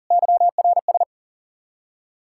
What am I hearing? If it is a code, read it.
Morse code, heard as a keyed tone, about 51 wpm, 700 Hz. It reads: YRS